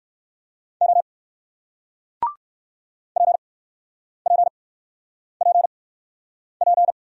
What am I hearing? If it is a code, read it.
Morse code, heard as a keyed tone, about 60 wpm, spaced as for 9 wpm, 700 Hz. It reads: O PPPP